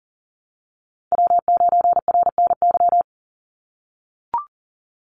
Morse code, heard as a keyed tone, about 41 wpm, 700 Hz. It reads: W9RNY
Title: Call Signs - 40wpm